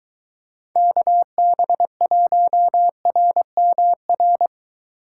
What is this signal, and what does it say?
Morse code, heard as a keyed tone, about 23 wpm, 700 Hz. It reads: KB1RMR